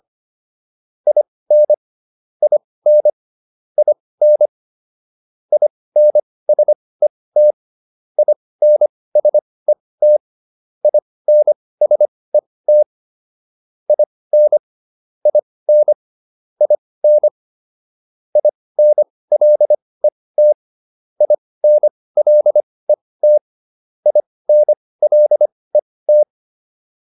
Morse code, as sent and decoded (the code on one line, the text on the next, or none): .. -. / .. -. / .. -. / .. -. ... - / .. -. ... - / .. -. ... - / .. -. / .. -. / .. -. / .. -. .-.. - / .. -. .-.. - / .. -. .-.. -
IN IN IN INSET INSET INSET IN IN IN INLET INLET INLET